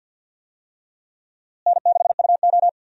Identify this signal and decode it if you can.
Morse code, heard as a keyed tone, about 47 wpm, 700 Hz. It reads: N6UO
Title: USA Call Signs - 4 Characters - Repeated 3x - 50wpm